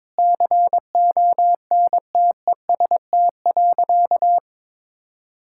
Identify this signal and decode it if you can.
Morse code, heard as a keyed tone, about 22 wpm, 700 Hz. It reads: CONTEST.
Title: Sentences from Top 1600 Words 22wpm